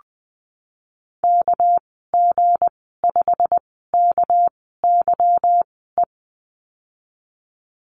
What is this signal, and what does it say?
Morse code, heard as a keyed tone, about 20 wpm, spaced as for 15 wpm, 700 Hz. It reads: KG5KYE
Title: 207 Lesson 33g - Letter j - Call Signs 20wpm